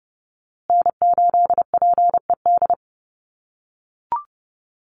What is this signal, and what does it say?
Morse code, heard as a keyed tone, about 30 wpm, 700 Hz. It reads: N8PED
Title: USA Call Signs Repeated 2x 30wpm